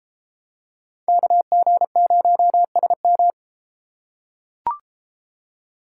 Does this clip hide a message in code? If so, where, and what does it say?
Yes: Morse code from 1.1 to 3.3 s, heard as a keyed tone, about 31 wpm, 700 Hz. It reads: KG0SM